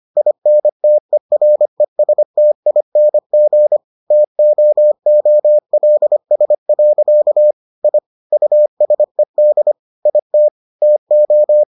Morse code, heard as a keyed tone, about 25 wpm, 600 Hz. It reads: INTERESTING TOOLS. I USED IT TO